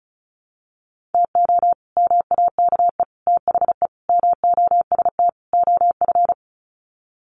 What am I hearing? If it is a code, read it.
Morse code, heard as a keyed tone, about 34 wpm, 700 Hz. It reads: TO MAKE THE MOST OF
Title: Sets of 5 Words Encore2 35wpm